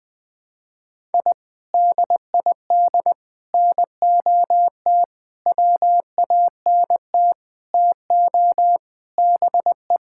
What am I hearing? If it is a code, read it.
Morse code, heard as a keyed tone, about 20 wpm, 700 Hz. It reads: I DID NOT WANT TO BE
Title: Sentences with 5-Word Phrases - 20wpm